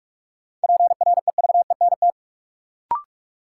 Morse code, heard as a keyed tone, about 46 wpm, 700 Hz. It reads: PREVENT
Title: Sentences from Top 1100 Words - New Word Drill Alt - 45wpm